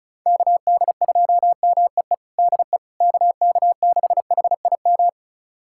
Morse code, heard as a keyed tone, about 34 wpm, 690 Hz. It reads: KD2MEE DE KK6HIM
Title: USA Callsign DE Callsign - 6 Characters - 35wpm